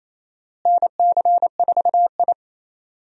Morse code, heard as a keyed tone, about 28 wpm, 700 Hz. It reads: NC4I
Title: USA Call Signs - 4 Characters - Repeated 2x - 28wpm